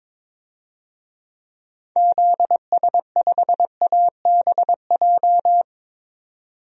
Morse code, heard as a keyed tone, about 22 wpm, 700 Hz. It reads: ZS5ABJ
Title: DXCC-100 Call Signs - 6 Characters - 22wpm